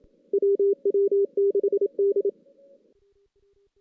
Morse code, heard as a keyed tone, about 27 wpm, 400 Hz. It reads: WW6D
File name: 2016 JAN CW NAQP / 2016 Jan naqp k5ot - 0-5